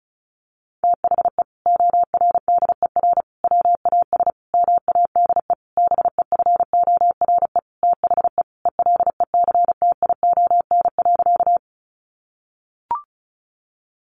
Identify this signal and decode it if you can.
Morse code, heard as a keyed tone, about 34 wpm, 700 Hz. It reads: THE ORDER WAS MADE BEFORE THE ELECTION.
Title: Sentences from Top 1500 Words 35wpm